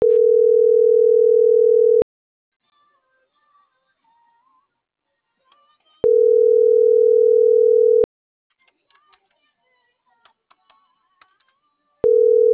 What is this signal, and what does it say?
Ringback tone (North America), 440 + 480 Hz.